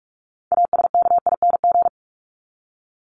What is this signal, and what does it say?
Morse code, heard as a keyed tone, about 44 wpm, 700 Hz. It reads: ASKING
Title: ING Suffix - Top 100 Words with Base Words 45wpm